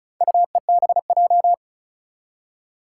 Morse code, heard as a keyed tone, about 36 wpm, 690 Hz. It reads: UEBJ
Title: Four Challenging Letter-Number Combinations 35wpm